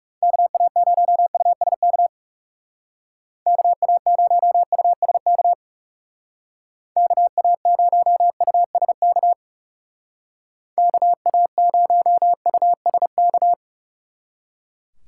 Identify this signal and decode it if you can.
Morse code, heard as a keyed tone, about 39 wpm, 700 Hz. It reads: KA0USK KA0USK KA0USK KA0USK